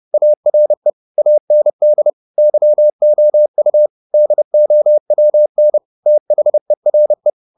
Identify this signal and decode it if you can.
Morse code, heard as a keyed tone, about 30 wpm, 600 Hz. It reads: ARE AND YOU DOWN THERE
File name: the old man and the sea / the old man and the sea 1 30wpm